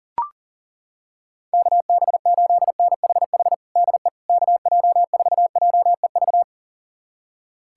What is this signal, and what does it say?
Morse code, heard as a keyed tone, about 40 wpm, 700 Hz. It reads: KB8NHH DE KJ4JEV
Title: USA Callsign DE Callsign 40wpm